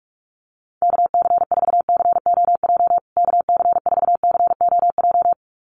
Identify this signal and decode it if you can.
Morse code, heard as a keyed tone, about 44 wpm, 700 Hz. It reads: KC4COJ KC4COJ
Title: USA Call Signs Repeated 2x 45wpm